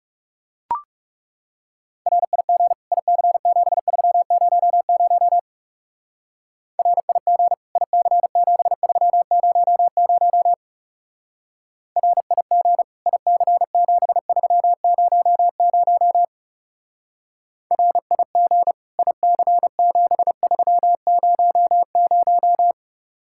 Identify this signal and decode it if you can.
Morse code, heard as a keyed tone, about 37 wpm, 700 Hz. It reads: RIG IC7300 RIG IC7300 RIG IC7300 RIG IC7300
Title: Speed-Racing Common QSO Phrases - 30wpm